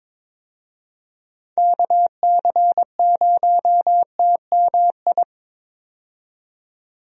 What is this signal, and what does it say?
Morse code, heard as a keyed tone, about 22 wpm, 700 Hz. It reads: KC0TMI
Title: USA Call Signs - 6 Characters - 22wpm